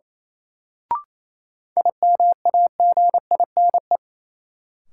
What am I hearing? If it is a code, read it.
Morse code, heard as a keyed tone, about 28 wpm, 700 Hz. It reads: IMAGINE
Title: Sentences from Top 1400 Words - New Word Drill - 28wpm